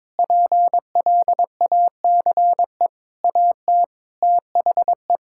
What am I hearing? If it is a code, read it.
Morse code, heard as a keyed tone, about 22 wpm, 700 Hz. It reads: PLACE AT THE
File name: Sets of 9 Words - 22wpm